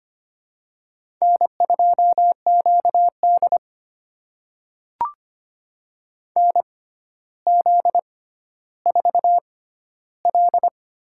Morse code, heard as a keyed tone, about 25 wpm, spaced as for 13 wpm, 700 Hz. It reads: N2QD NZ4L
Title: USA Call Signs - 4 Characters - 6x Character Spacing - 25wpm